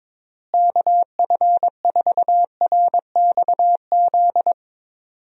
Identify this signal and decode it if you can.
Morse code, heard as a keyed tone, about 22 wpm, 700 Hz. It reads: KF4RXZ
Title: USA Call Signs - 6 Characters - Repeated 3x - 22wpm